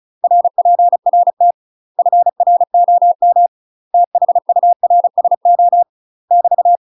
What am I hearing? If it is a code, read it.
Morse code, heard as a keyed tone, about 36 wpm, 700 Hz. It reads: RPRT FROM THURSO <BT>
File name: UK MASTER QSOs TRAINING / 35WPM / UK MASTER QSOs 0010